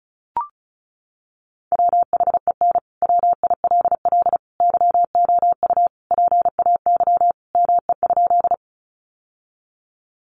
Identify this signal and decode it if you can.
Morse code, heard as a keyed tone, about 34 wpm, 700 Hz. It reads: WHEN WILL YOU PAY ME?
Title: Sentences from Top 700 Words 35wpm